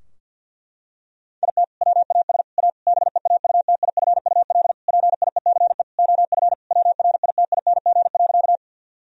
Morse code, heard as a keyed tone, about 52 wpm, 700 Hz. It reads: IT WAS A BEAUTIFUL PIECE OF WRITING.
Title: Sentences from Top 1000 Words - Review Part 2 - 50wpm